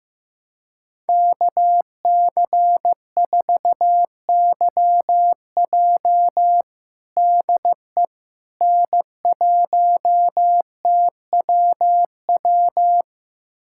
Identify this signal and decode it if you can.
Morse code, heard as a keyed tone, about 15 wpm, 700 Hz. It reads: KC4YJ DE N1TWW
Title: USA Callsign DE Callsign - 5 Characters - 15wpm